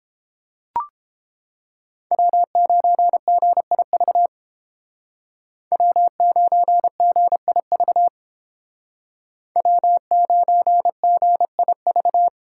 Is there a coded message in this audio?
Yes: Morse code from 2.1 to 12.3 s, heard as a keyed tone, about 31 wpm, 700 Hz. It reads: W9GIV W9GIV W9GIV